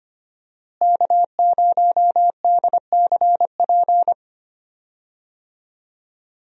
Morse code, heard as a keyed tone, about 25 wpm, 700 Hz. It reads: K0DCP
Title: USA Call Signs - 5 Characters - Repeated 2x - 25wpm